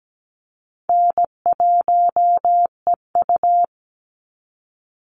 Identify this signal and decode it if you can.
Morse code, heard as a keyed tone, about 17 wpm, 700 Hz. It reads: N1EU
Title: NA SOTA Activator Callsigns 17wpm